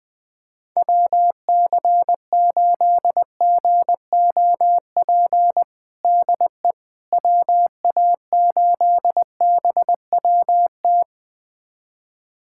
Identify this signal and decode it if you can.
Morse code, heard as a keyed tone, about 21 wpm, 700 Hz. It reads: WC8GOP DE WA8BWT
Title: USA Callsign DE Callsign 20wpm